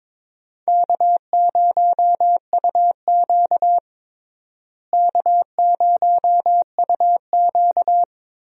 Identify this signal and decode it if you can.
Morse code, heard as a keyed tone, about 22 wpm, 700 Hz. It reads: K0UQ K0UQ
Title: USA Call Signs - 4 Characters - Repeated 3x - 22wpm